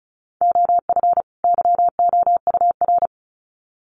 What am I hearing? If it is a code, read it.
Morse code, heard as a keyed tone, about 34 wpm, 700 Hz. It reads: OF YOUR
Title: Sets of 2 Words 35wpm